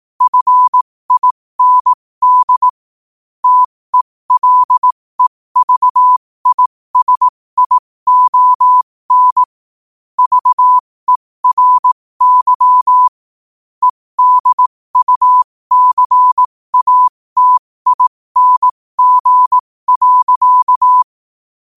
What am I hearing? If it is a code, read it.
Morse code, heard as a keyed tone, about 18 wpm, spaced as for 15 wpm, 1000 Hz. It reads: FIND TELEVISION VERY EDUCATING.
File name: qotdfiles / 15wpm